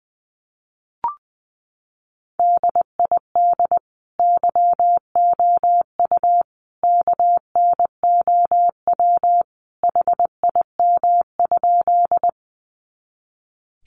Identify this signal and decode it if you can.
Morse code, heard as a keyed tone, about 20 wpm, 700 Hz. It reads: DID YOU KNOW HIM?
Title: Sentences from Top 100 Words 20wpm